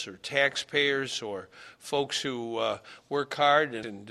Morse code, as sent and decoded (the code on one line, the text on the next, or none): none